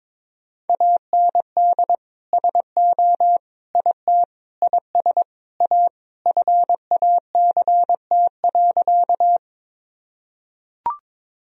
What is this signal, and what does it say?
Morse code, heard as a keyed tone, about 22 wpm, 700 Hz. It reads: AND SO IT IS A FACT.
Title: Sentences from Top 1000 Words - Review Part 3 - 22wpm